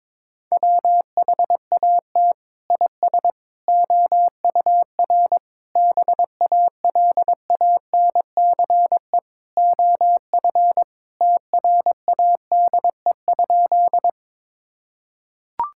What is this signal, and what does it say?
Morse code, heard as a keyed tone, about 22 wpm, 700 Hz. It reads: WHAT IS OUR BALANCE OF TRADE?
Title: Sentences from Top 1000 Words - Review Part 6 - 22wpm